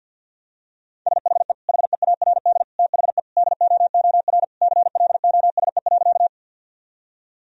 Morse code, heard as a keyed tone, about 49 wpm, 700 Hz. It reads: SHE HEARD THE DOOR CLOSE.
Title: Sentences from Top 1000 Words - Review Part 1 - 50wpm